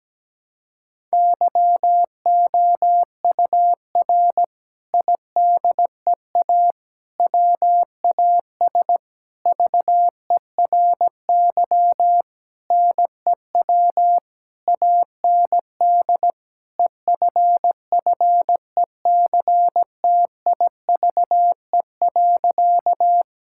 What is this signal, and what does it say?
Morse code, heard as a keyed tone, about 17 wpm, 700 Hz. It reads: YOUR IDEA WAS VERY NEW AND EFFECTIVE.
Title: Sentences from Top 1000 Words - Review Part 6 - 17wpm